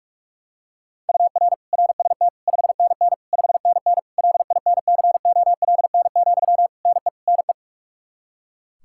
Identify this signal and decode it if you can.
Morse code, heard as a keyed tone, about 42 wpm, 700 Hz. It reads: UR RST 5NN 5NN LINCOLN, NE NE
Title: Practice QSO - Exchange 1 - Beginning - 45wpm